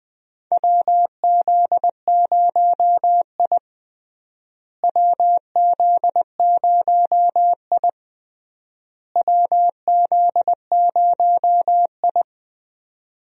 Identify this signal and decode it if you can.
Morse code, heard as a keyed tone, about 20 wpm, 700 Hz. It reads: WZ0I WZ0I WZ0I
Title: USA Call Signs - 4 Characters - Repeated 3x - 20wpm